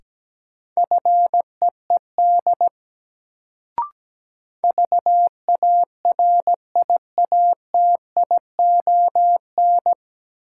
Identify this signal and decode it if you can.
Morse code, heard as a keyed tone, about 17 wpm, 700 Hz. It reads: FEED VARIATION
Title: Sentences from Top 1600 Words - New Word Drill Alt - 17wpm